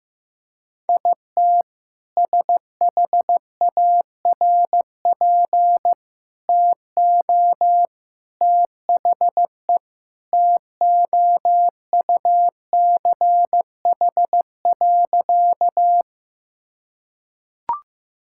Morse code, heard as a keyed tone, about 15 wpm, 700 Hz. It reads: IT SHARP TO THE TOUCH.
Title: Sentences from Top 1600 Words 15wpm